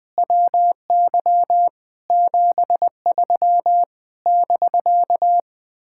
Morse code, heard as a keyed tone, about 20 wpm, 700 Hz. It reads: WY 73 <BK>
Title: Speed-Racing Common QSO Phrases - 20wpm